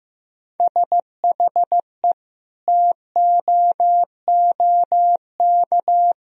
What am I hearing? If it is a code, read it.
Morse code, heard as a keyed tone, about 16 wpm, 700 Hz. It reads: SHE TOOK
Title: Sentences from Top 1000 Words - Review Part 1 - 15wpm